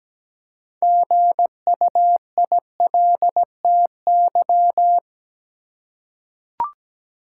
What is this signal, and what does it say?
Morse code, heard as a keyed tone, about 17 wpm, 700 Hz. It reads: GUILTY
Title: Six Letter Words - 17wpm